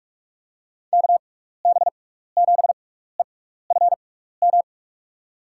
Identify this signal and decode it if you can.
Morse code, heard as a keyed tone, about 46 wpm, spaced as for 16 wpm, 700 Hz. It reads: KB7EFM
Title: USA Call Signs - 6x Character Spacing - 45wpm